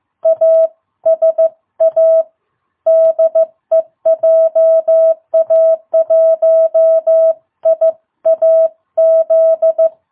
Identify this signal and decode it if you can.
Morse code, heard as a keyed tone, about 15 wpm, 660 Hz. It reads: ASA DE JA1IAZ
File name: jh8cbh / 2022cw